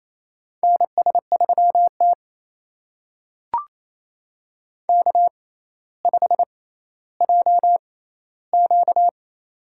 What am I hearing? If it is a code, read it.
Morse code, heard as a keyed tone, about 28 wpm, spaced as for 14 wpm, 700 Hz. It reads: NS3T K5JQ
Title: USA Call Signs - 4 Characters - 6x Character Spacing - 28wpm